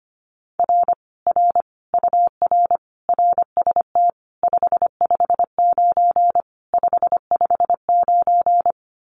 Morse code, heard as a keyed tone, about 25 wpm, 700 Hz. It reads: R R UR RST 559 559
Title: Common QSO Phrases Encore 25wpm